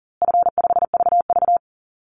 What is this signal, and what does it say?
Morse code, heard as a keyed tone, about 40 wpm, 700 Hz. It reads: F5VV
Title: Easy Call Signs Repeated 3x - 40wpm